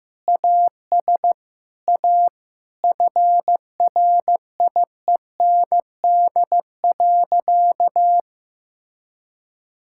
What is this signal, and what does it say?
Morse code, heard as a keyed tone, about 15 wpm, 700 Hz. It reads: AS A FRIEND.
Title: Sentences from Top 1000 Words - Review Part 6 - 15wpm